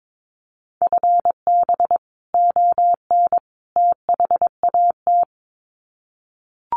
Morse code, heard as a keyed tone, about 22 wpm, 700 Hz. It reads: FB ON THAT
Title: Common QSO Phrases 22wpm